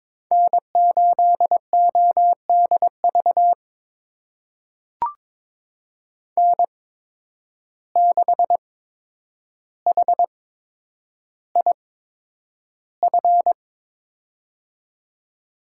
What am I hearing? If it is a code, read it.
Morse code, heard as a keyed tone, about 22 wpm, spaced as for 9 wpm, 700 Hz. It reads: N8ODV N6HIF
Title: USA Call Signs - 5 Characters - 8x Character Spacing - 22wpm